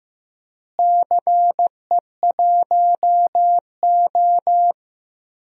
Morse code, heard as a keyed tone, about 15 wpm, 700 Hz. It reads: CE1O
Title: DXCC-100 Call Signs - 4 Characters - Repeated 3x - 15wpm